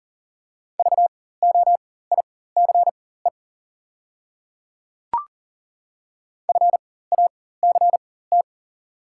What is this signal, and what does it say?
Morse code, heard as a keyed tone, about 38 wpm, spaced as for 18 wpm, 690 Hz. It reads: VOICE FACT
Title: 4x Character Spacing - Top 500 Words - 40wpm